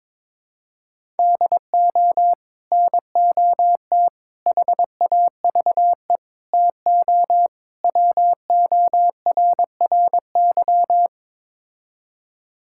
Morse code, heard as a keyed tone, about 22 wpm, 700 Hz. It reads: DO NOT HAVE TO WORRY
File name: practice / 22wpm / Build A Phrase - 5-Word Phrases - 22wpm